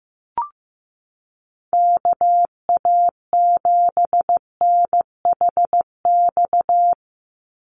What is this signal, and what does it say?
Morse code, heard as a keyed tone, about 15 wpm, 700 Hz. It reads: KA7NHX
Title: USA Call Signs 15wpm